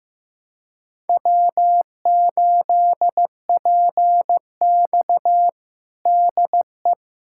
Morse code, heard as a keyed tone, about 15 wpm, 700 Hz. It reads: W8PX DE
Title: USA Callsign DE Callsign - 4 Characters - 15wpm